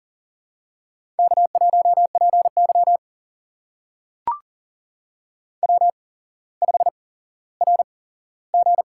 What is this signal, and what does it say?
Morse code, heard as a keyed tone, about 40 wpm, spaced as for 17 wpm, 700 Hz. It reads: K1PY W5RG
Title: USA Call Signs - 4 Characters - 8x Character Spacing - 40wpm